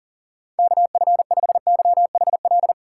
Morse code, heard as a keyed tone, about 40 wpm, 700 Hz. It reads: KF5YHL
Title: USA Call Signs - 6 Characters - 4x Character Spacing - 40wpm